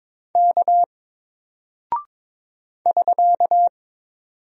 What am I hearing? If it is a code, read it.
Morse code, heard as a keyed tone, about 22 wpm, 700 Hz. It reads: K <SK>
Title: Single Character (Letter:Number:Punctuation:Prosign) 22wpm